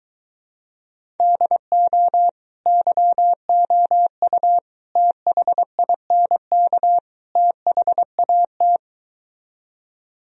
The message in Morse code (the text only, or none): DO YOU THINK THAT